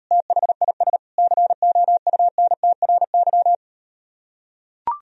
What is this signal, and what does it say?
Morse code, heard as a keyed tone, about 39 wpm, 700 Hz. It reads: THIS COUNTRY